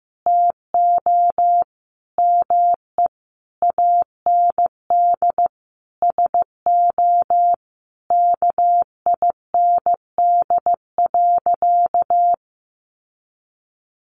Morse code, heard as a keyed tone, about 15 wpm, 700 Hz. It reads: TO ME AND SO KIND.
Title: Sentences from Top 1500 Words 15wpm